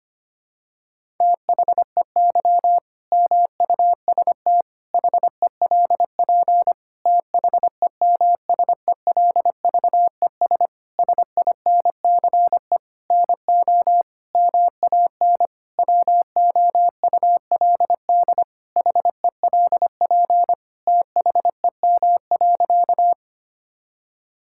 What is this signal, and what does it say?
Morse code, heard as a keyed tone, about 25 wpm, 700 Hz. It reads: THEY MUST HELP THEMSELVES SINCE NO MAN WOULD HELP THEM.
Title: Sentences from Top 1000 Words - Review Part 5 - 25wpm